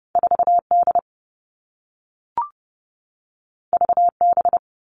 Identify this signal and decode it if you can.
Morse code, heard as a keyed tone, about 30 wpm, 700 Hz. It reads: 4D VB
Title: All Cut Numbers - 2-Digit 30wpm